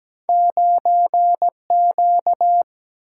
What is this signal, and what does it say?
Morse code, heard as a keyed tone, about 17 wpm, 700 Hz. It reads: 9 Q
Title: Single-Letter Single-Number 17wpm